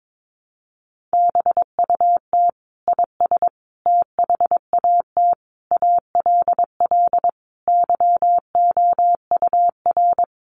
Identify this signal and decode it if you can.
Morse code, heard as a keyed tone, about 22 wpm, 700 Hz. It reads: BUT IS THAT ALL YOUR
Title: Sentences from Top 1300 Words 22wpm